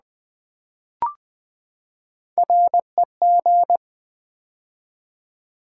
Morse code, heard as a keyed tone, about 20 wpm, 700 Hz. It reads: REG
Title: Build A Word - 3 Letters At A Time - 8-Letter Words - 20wpm